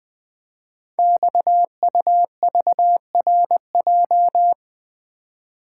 Morse code, heard as a keyed tone, about 20 wpm, 700 Hz. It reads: XUVRJ
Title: Code Groups - 5 Letters - 20wpm